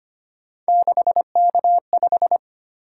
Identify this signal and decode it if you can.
Morse code, heard as a keyed tone, about 25 wpm, 700 Hz. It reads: 6K5
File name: Three Letter-Number Combinations - 25wpm